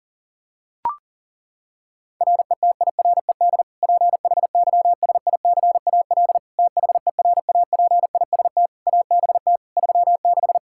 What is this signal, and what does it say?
Morse code, heard as a keyed tone, about 40 wpm, 700 Hz. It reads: RETIRED PHYSICAL THERAPIST ABT 36
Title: Practice QSO - Exchange 3 - More Info - 40wpm